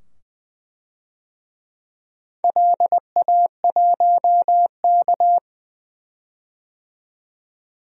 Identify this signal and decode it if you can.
Morse code, heard as a keyed tone, about 21 wpm, 700 Hz. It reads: LA1K
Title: DXCC-100 Call Signs - 4 Characters - 20wpm